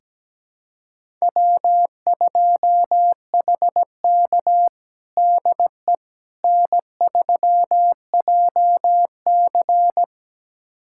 Morse code, heard as a keyed tone, about 17 wpm, 700 Hz. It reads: W2HK DE N3JC
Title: USA Callsign DE Callsign - 4 Characters - 17wpm